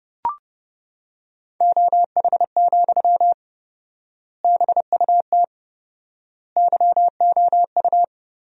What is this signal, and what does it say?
Morse code, heard as a keyed tone, about 30 wpm, 700 Hz. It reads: OH, BUT YOU
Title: 4x Word-Spacing - Sentences from Top 500 Words - 30wpm